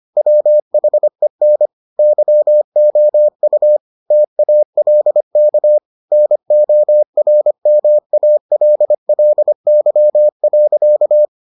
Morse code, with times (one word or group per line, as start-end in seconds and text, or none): 0.2-1.7 WHEN
2.0-3.8 YOU
4.1-5.8 TALK
6.1-11.2 NORMALLY.